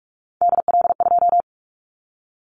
Morse code, heard as a keyed tone, about 44 wpm, 700 Hz. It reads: DL2
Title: Callsign Prefixes for Top 25 DXCC - 45wpm